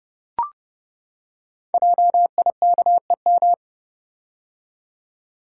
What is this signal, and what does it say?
Morse code, heard as a keyed tone, about 30 wpm, 700 Hz. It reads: JIKEM
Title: Code Groups - 5 Letters - 30wpm